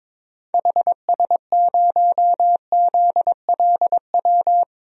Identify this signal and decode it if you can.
Morse code, heard as a keyed tone, about 21 wpm, 700 Hz. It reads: HS0ZLW
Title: DXCC-100 Call Signs - 6 Characters - Repeated 2x - 22wpm